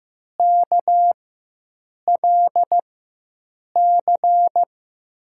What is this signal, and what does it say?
Morse code, heard as a keyed tone, about 15 wpm, spaced as for 7 wpm, 700 Hz. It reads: KLC
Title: USA Call Signs - 6 Characters - 4x Character Spacing - 15wpm